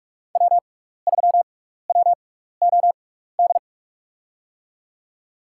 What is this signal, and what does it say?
Morse code, heard as a keyed tone, about 47 wpm, spaced as for 16 wpm, 700 Hz. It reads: W3WOD
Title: USA Call Signs - 5 Characters - 6x Character Spacing - 45wpm